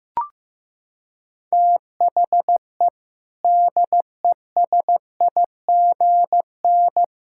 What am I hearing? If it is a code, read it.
Morse code, heard as a keyed tone, about 15 wpm, 700 Hz. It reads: THE DESIGN
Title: Sentences from Top 1000 Words - Review Part 5 - 15wpm